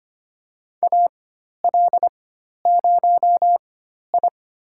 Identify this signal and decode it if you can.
Morse code, heard as a keyed tone, about 25 wpm, spaced as for 12 wpm, 700 Hz. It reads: AL0I
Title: USA Call Signs - 4x Character Spacing - 25wpm